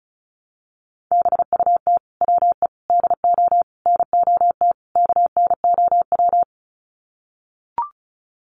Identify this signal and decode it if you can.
Morse code, heard as a keyed tone, about 34 wpm, 700 Hz. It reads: BUT WE DO NOT KNOW
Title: Sets of 5 Words Encore2 35wpm